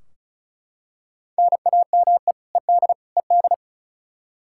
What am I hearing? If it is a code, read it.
Morse code, heard as a keyed tone, about 36 wpm, 700 Hz. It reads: NAME ED ED
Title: Practice QSO - Exchange 1 - Beginning - 35wpm